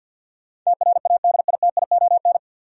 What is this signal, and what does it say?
Morse code, heard as a keyed tone, about 50 wpm, 700 Hz. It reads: TRADITION
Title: Sentences from Top 1500 Words - New Word Drill Alt - 50wpm